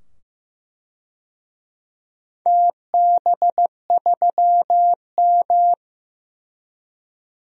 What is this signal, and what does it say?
Morse code, heard as a keyed tone, about 15 wpm, 700 Hz. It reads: TB3M
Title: DXCC-100 Call Signs - 4 Characters - 15wpm